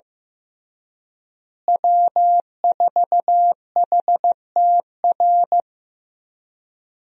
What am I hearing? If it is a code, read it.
Morse code, heard as a keyed tone, about 15 wpm, 700 Hz. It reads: W4HTR
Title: USA Call Signs - 5 Characters - Repeated 3x - 15wpm